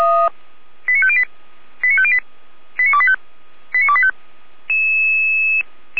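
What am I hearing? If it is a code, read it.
Five-tone selective call (ZVEI), heard as quick runs of five tones, tones of about 70 ms. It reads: 89508 89508 89285 89285
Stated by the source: ZVEI five-tone call 89001 received on SDR, fire brigade feuerwehr-orlamuende.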